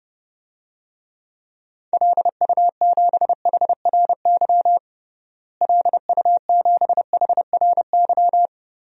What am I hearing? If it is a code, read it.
Morse code, heard as a keyed tone, about 30 wpm, 700 Hz. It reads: LU7HRY LU7HRY
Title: DXCC-100 Call Signs - 6 Characters - Repeated 3x - 30wpm